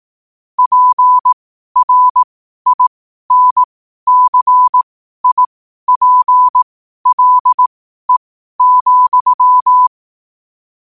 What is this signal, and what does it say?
Morse code, heard as a keyed tone, about 18 wpm, spaced as for 12 wpm, 1000 Hz. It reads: PRINCIPLE,
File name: qotdfiles / 12wpm